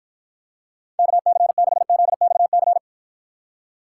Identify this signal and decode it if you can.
Morse code, heard as a keyed tone, about 52 wpm, 700 Hz. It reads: KC6ZXC